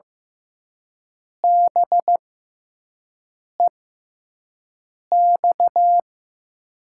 Morse code, heard as a keyed tone, about 15 wpm, spaced as for 5 wpm, 700 Hz. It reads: BEX
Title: USA Call Signs - 6 Characters - 6x Character Spacing - 15wpm